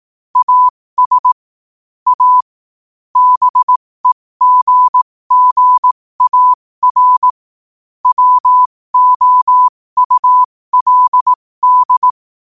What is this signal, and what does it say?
Morse code, heard as a keyed tone, about 18 wpm, spaced as for 15 wpm, 1000 Hz. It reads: AS A BEGGAR WOULD
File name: qotdfiles / 15wpm